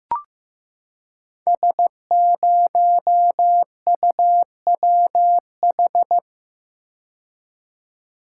Morse code, heard as a keyed tone, about 15 wpm, 700 Hz. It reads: S0UWH
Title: Code Groups - 5 Letter:Numbers - 15wpm